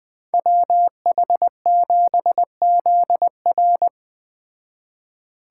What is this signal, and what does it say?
Morse code, heard as a keyed tone, about 20 wpm, 700 Hz. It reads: WH7ZR